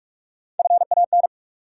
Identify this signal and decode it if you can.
Morse code, heard as a keyed tone, about 46 wpm, 700 Hz. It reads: FAN
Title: Sentences from Top 1000 Words - Review Part 4 - New Word Drill - 45wpm